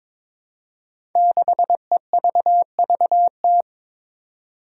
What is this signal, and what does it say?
Morse code, heard as a keyed tone, about 22 wpm, 700 Hz. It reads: 6EVVT
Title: Five Challenging Letter-Number Combinations 22wpm